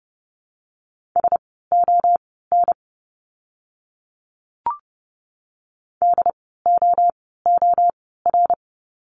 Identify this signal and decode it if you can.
Morse code, heard as a keyed tone, about 30 wpm, spaced as for 17 wpm, 700 Hz. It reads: SON DOOR
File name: Lesson 17f - Letter u - Words 30wpm